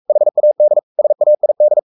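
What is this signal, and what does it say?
Morse code, heard as a keyed tone, about 44 wpm, 600 Hz. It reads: HAD SAID
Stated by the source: Morse code at 43 wpm sending Ken Follett's novel Never in English, from CW med Gnister.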